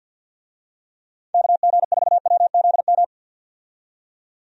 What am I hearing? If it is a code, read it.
Morse code, heard as a keyed tone, about 48 wpm, 700 Hz. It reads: KG4WZM
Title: USA Call Signs - 6 Characters - 50wpm